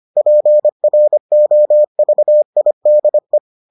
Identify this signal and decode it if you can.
Morse code, heard as a keyed tone, about 25 wpm, 600 Hz. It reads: PROVIDE